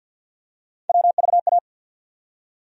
Morse code, heard as a keyed tone, about 47 wpm, 700 Hz. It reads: WVA
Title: USA Call Sign Postfixes - 3 Letters 50wpm